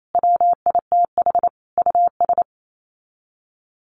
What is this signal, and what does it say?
Morse code, heard as a keyed tone, about 28 wpm, 700 Hz. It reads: WITH US
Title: Sets of 5 Words 28wpm